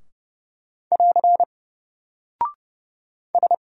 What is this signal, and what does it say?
Morse code, heard as a keyed tone, about 30 wpm, 700 Hz. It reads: <AR> S
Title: Single Character (Letter:Number:Punctuation:Prosign) 30wpm